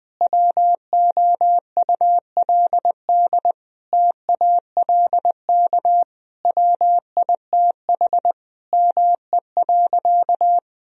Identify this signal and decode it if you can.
Morse code, heard as a keyed tone, about 20 wpm, 700 Hz. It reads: WOULD TALK WITH ME.
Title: Sentences from Top 1000 Words - Review Part 3 - 20wpm